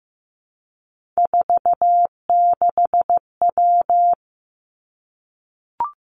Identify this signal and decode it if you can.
Morse code, heard as a keyed tone, about 15 wpm, 700 Hz. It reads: 46W
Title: Three Challenging Letter-Number Combinations 15wpm